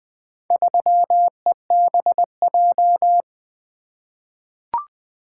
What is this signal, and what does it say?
Morse code, heard as a keyed tone, about 20 wpm, 700 Hz. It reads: 3EBJ
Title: Four Letter-Number Combinations - 20wpm